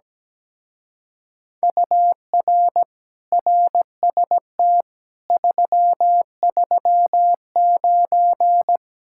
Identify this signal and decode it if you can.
Morse code, heard as a keyed tone, about 17 wpm, 700 Hz. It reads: UR RST 339
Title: Speed-Racing Common QSO Phrases - 17wpm